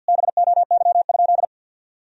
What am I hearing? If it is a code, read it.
Morse code, heard as a keyed tone, about 50 wpm, 700 Hz. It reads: BOY?
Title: Sentences from Top 1000 Words - Review Part 4 - 50wpm